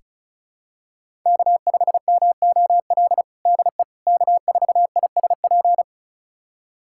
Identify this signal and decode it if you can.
Morse code, heard as a keyed tone, about 36 wpm, 700 Hz. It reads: K5MOL DE K4ISP
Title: USA Callsign DE Callsign - 5 Characters - 35wpm